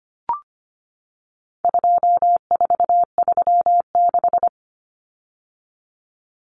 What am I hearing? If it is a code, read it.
Morse code, heard as a keyed tone, about 25 wpm, 700 Hz. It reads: 2436
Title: Numbers - 4-Digit 25wpm